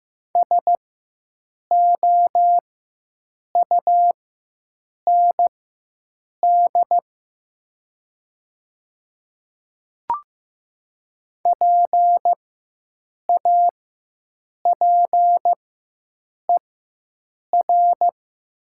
Morse code, heard as a keyed tone, about 15 wpm, spaced as for 7 wpm, 700 Hz. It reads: SOUND PAPER
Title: 4x Character Spacing - Top 300 Words - 15wpm